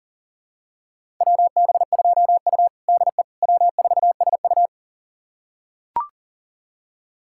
Morse code, heard as a keyed tone, about 40 wpm, 700 Hz. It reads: WB2U DE W4SU
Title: USA Callsign DE Callsign - 4 Characters - 40wpm